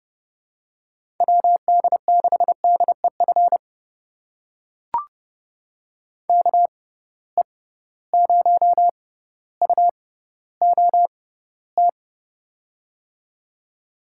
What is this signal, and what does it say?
Morse code, heard as a keyed tone, about 29 wpm, spaced as for 15 wpm, 700 Hz. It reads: WD6DEF KE0UOT
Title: USA Call Signs - 6 Characters - 6x Character Spacing - 30wpm